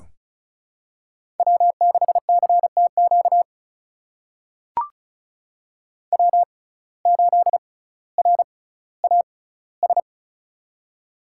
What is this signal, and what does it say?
Morse code, heard as a keyed tone, about 36 wpm, spaced as for 18 wpm, 700 Hz. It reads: W6CTQ W8RAS
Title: USA Call Signs - 5 Characters - 6x Character Spacing - 35wpm